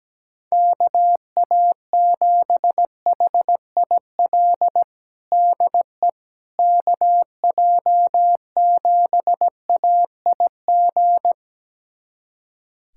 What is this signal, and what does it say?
Morse code, heard as a keyed tone, about 17 wpm, 700 Hz. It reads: KA7HIL DE KJ7AIG
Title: USA Callsign DE Callsign - 6 Characters - 17wpm